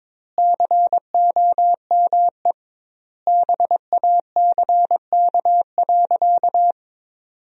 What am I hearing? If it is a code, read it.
Morse code, heard as a keyed tone, about 22 wpm, 700 Hz. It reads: COME BACK.
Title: 2x Word-Spacing - Sentences from Top 200 Words - 22wpm